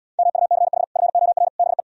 Morse code, heard as a keyed tone, about 78 wpm, 700 Hz. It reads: MUCH FOR BE